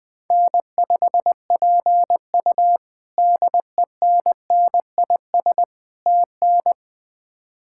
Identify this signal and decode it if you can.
Morse code, heard as a keyed tone, about 21 wpm, 700 Hz. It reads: N5PU DENNIS TN
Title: K1USN SST QSO Practice - 20wpm